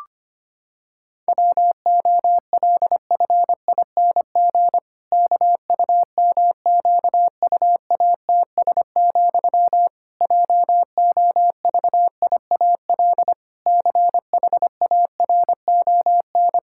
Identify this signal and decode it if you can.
Morse code, heard as a keyed tone, about 25 wpm, 700 Hz. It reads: WOLFING KUMQUATS, JOVIAL CHARON
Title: Pangram Challenge v2 - 25wpm